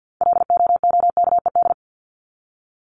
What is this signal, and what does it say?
Morse code, heard as a keyed tone, about 50 wpm, 700 Hz. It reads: LOOKED
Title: ED Suffix - Top 100 Words with Base Words 50wpm